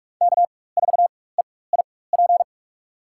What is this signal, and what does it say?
Morse code, heard as a keyed tone, about 46 wpm, spaced as for 21 wpm, 700 Hz. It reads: K4EIP
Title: USA Call Signs - 5 Characters - 4x Character Spacing - 45wpm